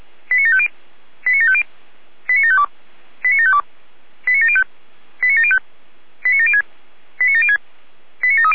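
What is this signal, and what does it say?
Five-tone selective call (ZVEI), heard as quick runs of five tones, tones of about 80 ms. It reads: 89750 89750 89752 89752 89705 89705 89706 89706 89730
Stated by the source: ZVEI five-tone selective call received on an SDR.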